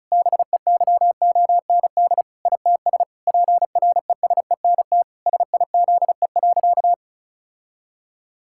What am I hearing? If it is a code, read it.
Morse code, heard as a keyed tone, about 34 wpm, 700 Hz. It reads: BEYOND ITS PRESENT SIZE.
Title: Sentences from Top 1000 Words - Review Part 6 - 35wpm